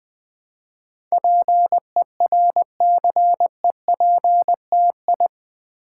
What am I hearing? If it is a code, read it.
Morse code, heard as a keyed tone, about 20 wpm, 700 Hz. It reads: PERCEPTI
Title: Build A Word - 4 Letters At A Time - 10-Letter Words - 20wpm